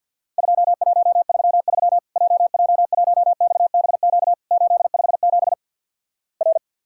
Morse code, heard as a keyed tone, about 54 wpm, 700 Hz. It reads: 2133 JJ1XBQ 857 R
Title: Worked All Europe DX Contest - QTC Exchange with Repeats - 50wpm